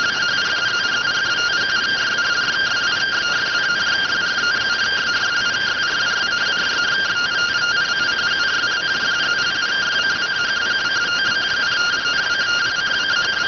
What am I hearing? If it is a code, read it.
Two-tone teleprinter signal, about 170 Hz shift.